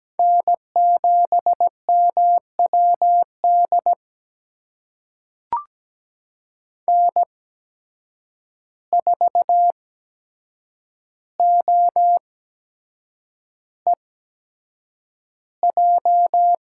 Morse code, heard as a keyed tone, about 17 wpm, spaced as for 7 wpm, 700 Hz. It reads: N7MWD N4OEJ
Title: USA Call Signs - 8x Character Spacing - 17wpm